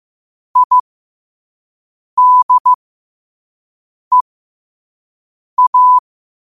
Morse code, heard as a keyed tone, about 15 wpm, spaced as for 5 wpm, 1000 Hz. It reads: IDEA
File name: qotdfiles / 5wpm